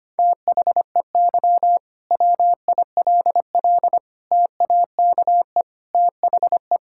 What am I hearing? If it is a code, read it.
Morse code, heard as a keyed tone, about 25 wpm, 700 Hz. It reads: THEY WILL TAKE THE
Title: Sentences from Top 1000 Words - Review Part 5 - 25wpm